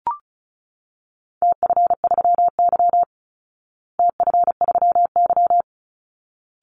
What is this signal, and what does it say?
Morse code, heard as a keyed tone, about 34 wpm, 700 Hz. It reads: TF3Y TF3Y
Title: Call Signs Repeated 3x - 35wpm